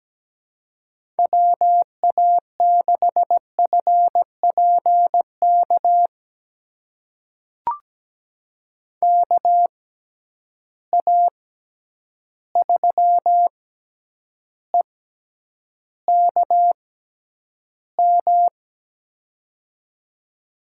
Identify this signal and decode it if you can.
Morse code, heard as a keyed tone, about 17 wpm, spaced as for 9 wpm, 700 Hz. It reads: WA6FPK KA3EKM
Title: USA Call Signs - 6 Characters - 6x Character Spacing - 17wpm